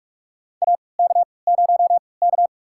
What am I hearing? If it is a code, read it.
Morse code, heard as a keyed tone, about 44 wpm, spaced as for 25 wpm, 700 Hz. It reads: AK0K
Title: USA Call Signs - 4 Characters - 3x Character Spacing - 45wpm